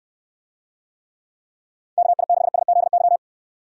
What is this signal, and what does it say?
Morse code, heard as a keyed tone, about 69 wpm, 700 Hz. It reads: KI6SZY